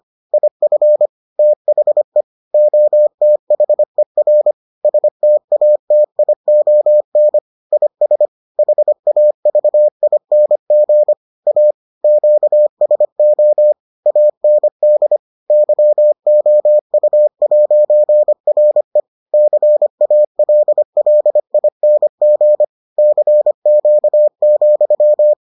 Morse code, heard as a keyed tone, about 25 wpm, 600 Hz. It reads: IF THE OTHER STATION IS HAVING A QSO AND YOU'RE CALLING CQ,